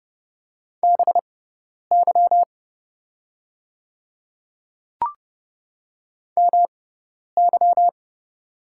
Morse code, heard as a keyed tone, about 30 wpm, spaced as for 10 wpm, 700 Hz. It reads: BY MY
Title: 6x Character Spacing - Top 100 Words - 30wpm